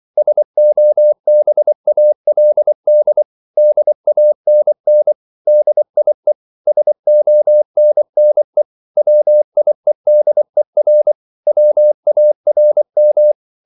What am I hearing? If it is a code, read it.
Morse code, heard as a keyed tone, about 24 wpm, 600 Hz. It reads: SOBALD DANN DIE SONNE WIEDER WARM